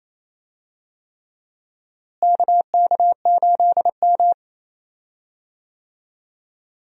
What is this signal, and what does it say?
Morse code, heard as a keyed tone, about 27 wpm, 700 Hz. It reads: KK8M
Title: USA Call Signs - 4 Characters - Repeated 3x - 28wpm